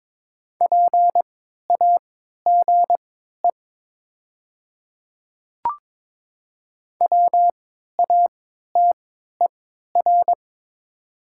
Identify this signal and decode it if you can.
Morse code, heard as a keyed tone, about 22 wpm, spaced as for 12 wpm, 700 Hz. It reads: PAGE WATER
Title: 3x Character Spacing - Top 300 Words - 22wpm